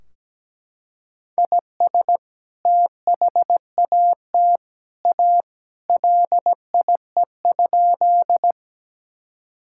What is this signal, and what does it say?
Morse code, heard as a keyed tone, about 17 wpm, 700 Hz. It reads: IS THAT A LIE?
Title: Sentences from Top 1600 Words 17wpm